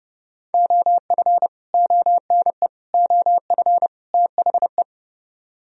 Morse code, heard as a keyed tone, about 31 wpm, 700 Hz. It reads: OF ONE OF THE